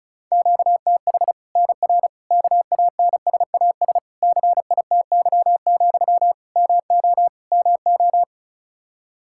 Morse code, heard as a keyed tone, about 34 wpm, 700 Hz. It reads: QTH NR KANSAS CITY, MO MO
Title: Practice QSO - Exchange 1 - Beginning - 35wpm